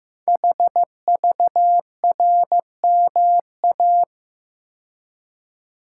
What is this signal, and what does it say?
Morse code, heard as a keyed tone, about 15 wpm, 700 Hz. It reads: HVRMA
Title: Code Groups - 5 Letters - 15wpm